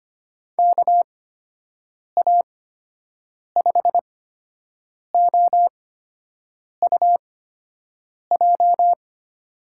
Morse code, heard as a keyed tone, about 25 wpm, spaced as for 7 wpm, 700 Hz. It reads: KA5OUJ
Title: USA Call Signs - 6 Characters - 8x Character Spacing - 25wpm